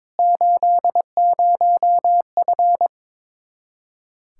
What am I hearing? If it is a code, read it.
Morse code, heard as a keyed tone, about 22 wpm, 700 Hz. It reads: 80F
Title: K1USN SST QSO Practice 22wpm